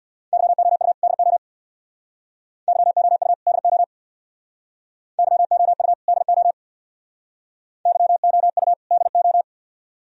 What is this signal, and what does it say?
Morse code, heard as a keyed tone, about 62 wpm, 700 Hz. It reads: YOU DO YOU DO YOU DO YOU DO
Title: Speed-Racing - Sets of 2 Words 50wpm